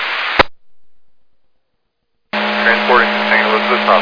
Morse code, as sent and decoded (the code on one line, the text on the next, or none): none